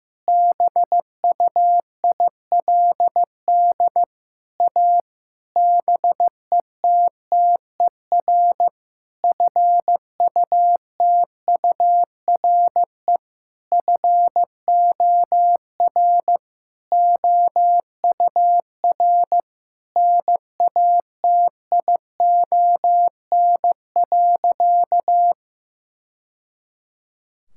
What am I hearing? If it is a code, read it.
Morse code, heard as a keyed tone, about 15 wpm, 700 Hz. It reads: BUILD A BETTER FUTURE FOR OUR NATION.
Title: Sentences from Top 1000 Words - Review Part 6 - 15wpm